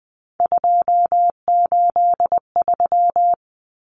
Morse code, heard as a keyed tone, about 20 wpm, 700 Hz. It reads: 283
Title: Common Cut Numbers - 3-Digit 20wpm